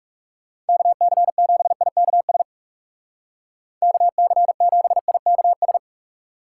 Morse code, heard as a keyed tone, about 36 wpm, 700 Hz. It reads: KC7IKS KC7IKS
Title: Speed Racing - USA 2x3 Call Signs 30wpm